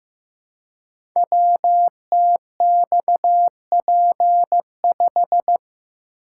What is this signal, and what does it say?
Morse code, heard as a keyed tone, about 15 wpm, 700 Hz. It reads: WTXP5
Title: Code Groups - 5 Letter:Numbers - 15wpm